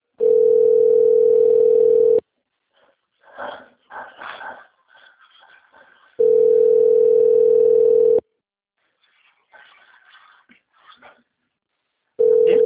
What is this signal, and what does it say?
Ringback tone (North America), 440 + 480 Hz.